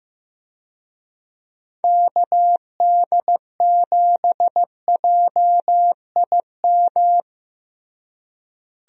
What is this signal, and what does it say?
Morse code, heard as a keyed tone, about 15 wpm, 700 Hz. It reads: KD7JIM
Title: USA Call Signs - 6 Characters - Repeated 3x - 15wpm